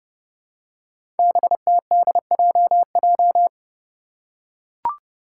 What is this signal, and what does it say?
Morse code, heard as a keyed tone, about 30 wpm, 700 Hz. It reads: BTDJJ
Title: Five Challenging Letter-Number Combinations 30wpm